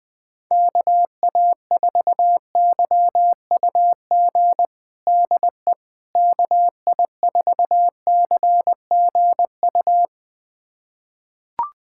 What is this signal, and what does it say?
Morse code, heard as a keyed tone, about 20 wpm, 700 Hz. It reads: KA4YUG DE KI4CGU
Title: USA Callsign DE Callsign - 6 Characters - 20wpm